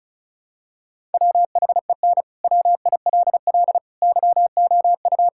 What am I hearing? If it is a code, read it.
Morse code, heard as a keyed tone, about 36 wpm, 700 Hz. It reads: WHEN WILL YOU
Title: Sentences from Top 1000 Words - Review Part 6 - 35wpm